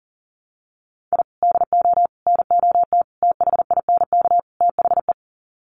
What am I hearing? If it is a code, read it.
Morse code, heard as a keyed tone, about 40 wpm, 700 Hz. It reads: I DO NOT THINK THE
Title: Sets of 5 Words 40wpm